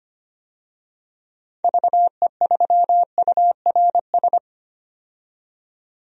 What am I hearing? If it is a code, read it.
Morse code, heard as a keyed tone, about 25 wpm, 700 Hz. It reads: VE3URS
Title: DXCC-100 Call Signs - 6 Characters - 25wpm